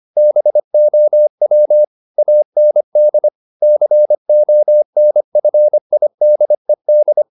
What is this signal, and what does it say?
Morse code, heard as a keyed tone, about 25 wpm, 600 Hz. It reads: BOW AND CONFIDED